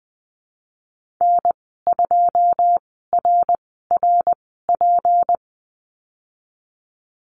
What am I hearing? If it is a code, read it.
Morse code, heard as a keyed tone, about 20 wpm, spaced as for 15 wpm, 700 Hz. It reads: N2RRP
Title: 161 Lesson 26g - Letter g - Call Signs 20wpm